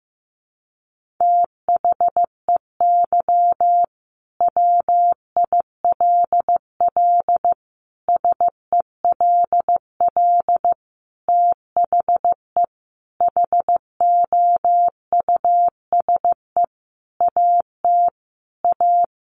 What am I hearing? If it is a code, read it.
Morse code, heard as a keyed tone, about 15 wpm, 700 Hz. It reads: THEY WILL SELL THE HOUSE AT A
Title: Sentences from Top 1200 Words 15wpm